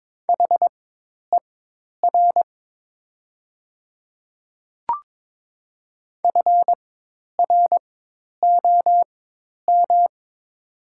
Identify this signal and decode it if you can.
Morse code, heard as a keyed tone, about 22 wpm, spaced as for 10 wpm, 700 Hz. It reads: HER FROM
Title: 4x Character Spacing - Top 100 Words - 22wpm